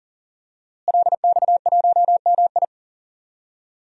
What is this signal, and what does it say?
Morse code, heard as a keyed tone, about 37 wpm, 700 Hz. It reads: LX1MI
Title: DXCC-100 Call Signs - 5 Characters - Repeated 2x - 40wpm